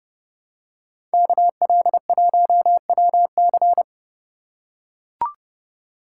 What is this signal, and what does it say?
Morse code, heard as a keyed tone, about 29 wpm, 700 Hz. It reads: KL1WC